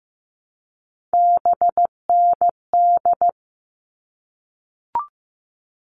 Morse code, heard as a keyed tone, about 15 wpm, 700 Hz. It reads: BND
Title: All Cut Numbers - 3-Digit 15wpm